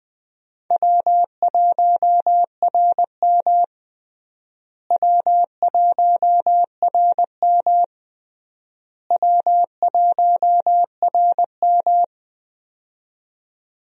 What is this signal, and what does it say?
Morse code, heard as a keyed tone, about 20 wpm, 700 Hz. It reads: W1RM W1RM W1RM
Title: USA Call Signs - 4 Characters - Repeated 3x - 20wpm